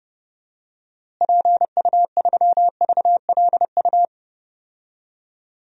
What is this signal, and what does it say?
Morse code, heard as a keyed tone, about 29 wpm, 700 Hz. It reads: PU3VLU
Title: DXCC-100 Call Signs - 6 Characters - Repeated 2x - 30wpm